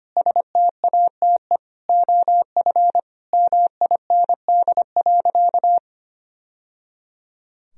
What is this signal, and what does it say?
Morse code, heard as a keyed tone, about 25 wpm, 700 Hz. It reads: STATE OF MIND.
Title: Sentences from Top 1000 Words - Review Part 2 - 25wpm